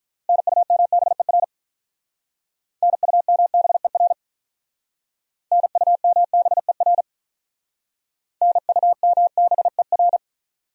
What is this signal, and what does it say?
Morse code, heard as a keyed tone, about 47 wpm, 700 Hz. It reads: NUMBER NUMBER NUMBER NUMBER